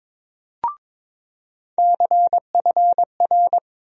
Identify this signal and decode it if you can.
Morse code, heard as a keyed tone, about 22 wpm, 700 Hz. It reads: CFR
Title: USA Call Sign Postfixes - 3 Letters 22wpm